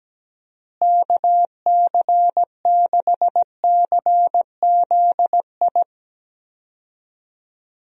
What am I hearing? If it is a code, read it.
Morse code, heard as a keyed tone, about 17 wpm, 700 Hz. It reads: KC6CZI
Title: USA Call Signs - 6 Characters - Repeated 2x - 17wpm